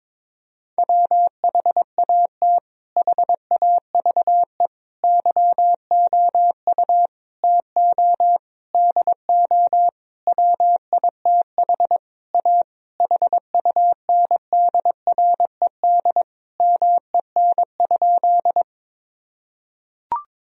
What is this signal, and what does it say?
Morse code, heard as a keyed tone, about 22 wpm, 700 Hz. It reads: WHAT HAVE YOU TO DO WITH A HUNDRED MEN?
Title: Sentences from Top 1000 Words - Review Part 2 - 22wpm